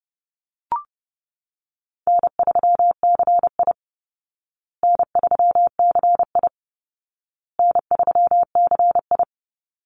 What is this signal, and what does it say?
Morse code, heard as a keyed tone, about 30 wpm, 700 Hz. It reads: N3CI N3CI N3CI
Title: Easy Call Signs Repeated 3x - 30wpm